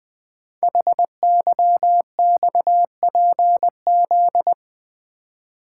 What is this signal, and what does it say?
Morse code, heard as a keyed tone, about 20 wpm, 700 Hz. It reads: HYXPZ
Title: Code Groups - 5 Letters - 20wpm